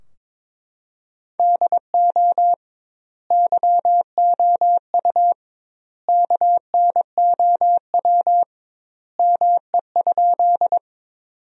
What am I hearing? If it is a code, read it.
Morse code, heard as a keyed tone, about 22 wpm, 700 Hz. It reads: DO YOU KNOW ME?
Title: 2x Word-Spacing - Sentences from Top 200 Words - 22wpm